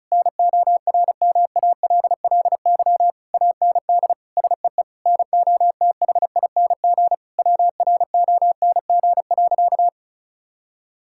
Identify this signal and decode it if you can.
Morse code, heard as a keyed tone, about 36 wpm, 690 Hz. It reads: NORMALLY AND SEE NOTHING WRONG.
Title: Sentences from Top 1000 Words - Review Part 6 - 35wpm